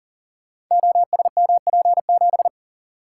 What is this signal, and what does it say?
Morse code, heard as a keyed tone, about 38 wpm, 700 Hz. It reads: OSMP7
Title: Code Groups - 5 Letter:Numbers - 40wpm